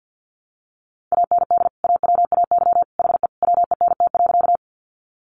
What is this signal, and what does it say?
Morse code, heard as a keyed tone, about 50 wpm, 700 Hz. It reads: AND AWAY HE WENT.
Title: Sentences from Top 600 Words 50wpm